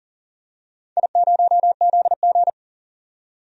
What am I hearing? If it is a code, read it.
Morse code, heard as a keyed tone, about 40 wpm, 700 Hz. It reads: I0ZG
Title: DXCC-100 Call Signs - 4 Characters - 40wpm